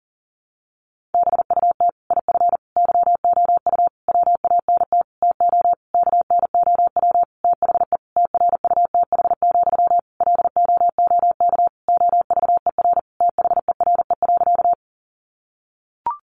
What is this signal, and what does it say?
Morse code, heard as a keyed tone, about 40 wpm, 700 Hz. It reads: BUT IF YOU WANT TO KNOW THE TRUTH, LOOK OVER THERE.
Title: Sentences from Top 800 Words 40wpm